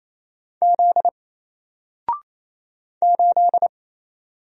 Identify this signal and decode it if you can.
Morse code, heard as a keyed tone, about 27 wpm, 700 Hz. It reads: Z 8
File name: Single-Letter-Number-28wpm